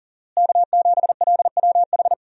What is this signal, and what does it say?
Morse code, heard as a keyed tone, about 37 wpm, 700 Hz. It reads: K7LWH
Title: USA Call Signs - 5 Characters - Repeated 3x - 40wpm